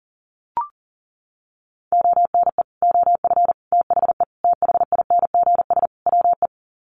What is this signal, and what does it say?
Morse code, heard as a keyed tone, about 40 wpm, 700 Hz. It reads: ONE OF THE THINGS WE
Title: Sets of 5 Words Encore 40wpm